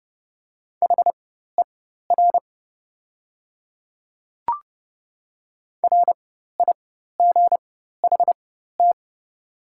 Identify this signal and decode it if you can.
Morse code, heard as a keyed tone, about 30 wpm, spaced as for 14 wpm, 690 Hz. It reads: HER RIGHT
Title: 4x Character Spacing - Top 500 Words - 30wpm